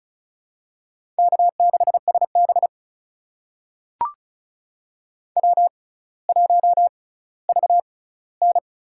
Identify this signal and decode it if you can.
Morse code, heard as a keyed tone, about 36 wpm, spaced as for 18 wpm, 700 Hz. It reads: K6SB W1VN
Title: USA Call Signs - 4 Characters - 6x Character Spacing - 35wpm